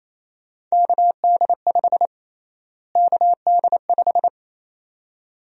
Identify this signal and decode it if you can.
Morse code, heard as a keyed tone, about 28 wpm, 700 Hz. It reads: KD5 KD5
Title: USA Call Sign Prefixes - Repeated 2x - 28wpm